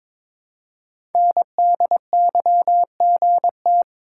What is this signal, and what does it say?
Morse code, heard as a keyed tone, about 22 wpm, 700 Hz. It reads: NDYGT
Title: Code Groups - 5 Letter:Numbers - 22wpm